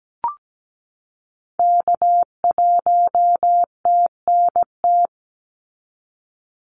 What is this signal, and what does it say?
Morse code, heard as a keyed tone, about 17 wpm, 700 Hz. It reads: K1TNT
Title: NA SOTA Activator Callsigns 17wpm